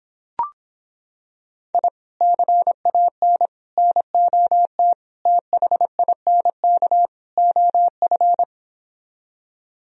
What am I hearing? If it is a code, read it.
Morse code, heard as a keyed tone, about 26 wpm, 700 Hz. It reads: I CAN NOT THINK OF